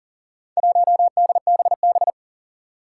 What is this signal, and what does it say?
Morse code, heard as a keyed tone, about 40 wpm, 700 Hz. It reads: 1DBB
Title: Four Challenging Letter-Number Combinations 40wpm